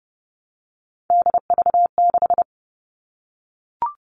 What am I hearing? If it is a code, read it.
Morse code, heard as a keyed tone, about 29 wpm, 700 Hz. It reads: DV6
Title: All Cut Numbers - 3-Digit 30wpm